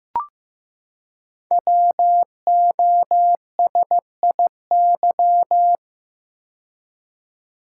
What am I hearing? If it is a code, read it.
Morse code, heard as a keyed tone, about 15 wpm, 700 Hz. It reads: WOSIY
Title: Code Groups - 5 Letter:Numbers - 15wpm